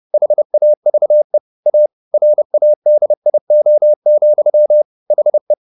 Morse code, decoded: HAVE A RADIO, HE

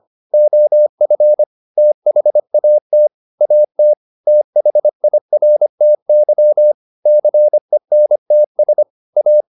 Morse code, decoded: OF THAT AT THIRTY CENTS A

